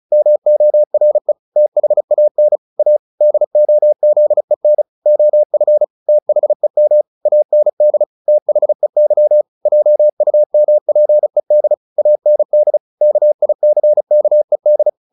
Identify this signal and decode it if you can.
Morse code, heard as a keyed tone, about 34 wpm, 600 Hz. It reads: MORE THAN A DOZEN OF THEM AND THEY JUMPED AND KICKED
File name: the old man and the sea 2 35wpm